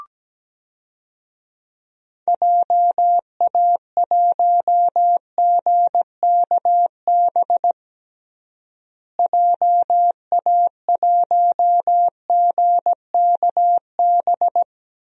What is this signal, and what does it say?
Morse code, heard as a keyed tone, about 17 wpm, 700 Hz. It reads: JA1GKB JA1GKB
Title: DXCC-100 Call Signs - 6 Characters - Repeated 2x - 17wpm